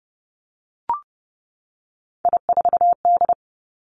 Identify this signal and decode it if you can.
Morse code, heard as a keyed tone, about 30 wpm, 700 Hz. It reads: I4D
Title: Three Challenging Letter-Number Combinations 30wpm